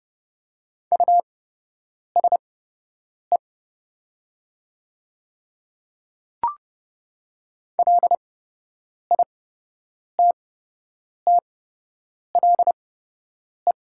Morse code, heard as a keyed tone, about 29 wpm, spaced as for 8 wpm, 700 Hz. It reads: USE LITTLE
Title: 8x Character Spacing - Top 200 Words - 30wpm